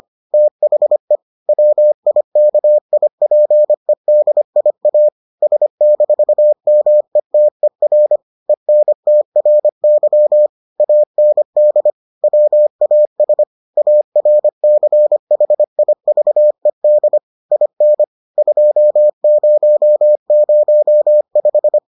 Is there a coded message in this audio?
Yes: Morse code from 0.3 to 21.8 s, heard as a keyed tone, about 25 wpm, 600 Hz. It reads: THE WIKIPEDIA S-METER ENTRY AND WAS ARCHIVED IN 2005